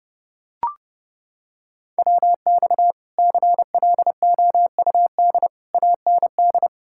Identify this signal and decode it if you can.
Morse code, heard as a keyed tone, about 29 wpm, 700 Hz. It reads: WX CLOUD AND